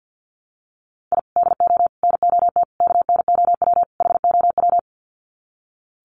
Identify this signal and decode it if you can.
Morse code, heard as a keyed tone, about 50 wpm, 700 Hz. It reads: I DO NOT KNOW HOW
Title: Sets of 5 Words 50wpm